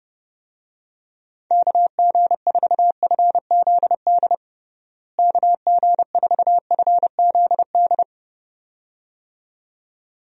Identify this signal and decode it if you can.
Morse code, heard as a keyed tone, about 30 wpm, 700 Hz. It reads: KG4FZD KG4FZD
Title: USA Call Signs - 6 Characters - Repeated 2x - 30wpm